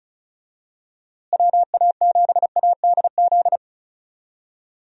Morse code, heard as a keyed tone, about 36 wpm, 700 Hz. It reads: WA7ADZ
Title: USA Call Signs - 6 Characters - Repeated 2x - 35wpm